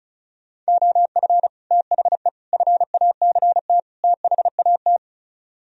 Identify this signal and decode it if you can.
Morse code, heard as a keyed tone, about 40 wpm, 700 Hz. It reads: OF THE FACT THAT